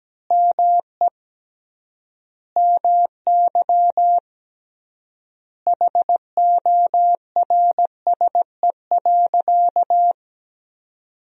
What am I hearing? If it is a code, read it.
Morse code, heard as a keyed tone, about 17 wpm, 700 Hz. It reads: ME MY HORSE.
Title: 3x Word-Spacing - Sentences from Top 500 Words - 17wpm